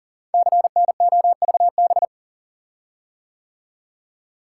Morse code, heard as a keyed tone, about 39 wpm, 700 Hz. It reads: CNOVB
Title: Code Groups - 5 Letters - 40wpm